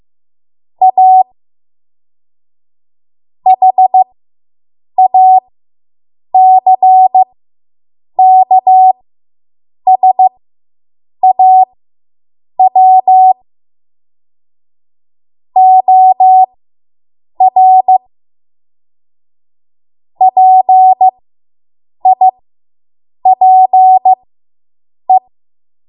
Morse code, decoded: A HACKSAW OR PIPE

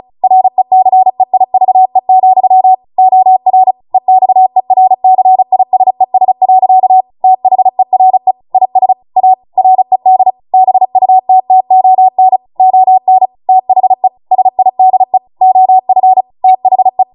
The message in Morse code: RECEIVE, OR EXERCISES. THERE IS A RED BUTTON ON THE SIDE OF THE

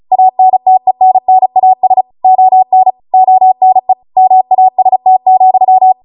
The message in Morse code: ANTENNAS ON ONE MAST,

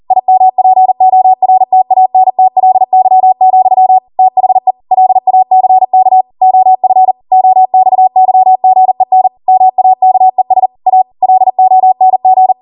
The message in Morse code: IMPORTANTLY, THE LACK OF OXYGEN MAKES A LONG

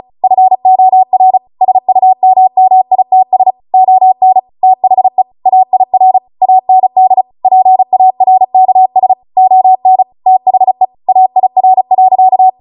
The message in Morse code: FOR SUMMITS ON THE AIR AND PARKS ON THE AIR.

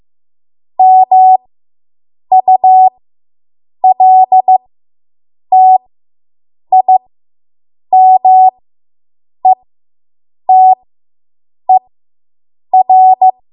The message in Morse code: MULTIMETER

